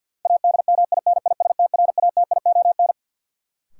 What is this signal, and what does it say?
Morse code, heard as a keyed tone, about 53 wpm, 700 Hz. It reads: ADMINISTRATION
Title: Sentences from Top 1300 Words - New Word Drill Alt - 50wpm